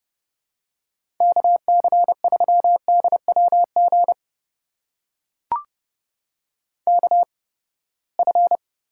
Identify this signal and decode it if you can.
Morse code, heard as a keyed tone, about 30 wpm, spaced as for 21 wpm, 700 Hz. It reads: KC3DWG KF
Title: USA Call Signs - 8x Character Spacing - 30wpm